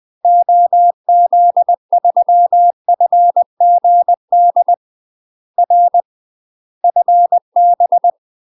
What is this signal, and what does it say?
Morse code, heard as a keyed tone, about 20 wpm, 700 Hz. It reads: OZ3FGD R FB
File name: UK MASTER QSOs TRAINING / 20WPM / UK MASTER QSOs 0020